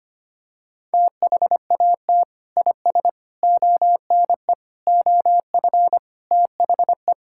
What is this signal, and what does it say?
Morse code, heard as a keyed tone, about 25 wpm, 700 Hz. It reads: THAT IS ONE OF THE
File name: practice / 25wpm / Sentences with 4-Word Phrases - 25wpm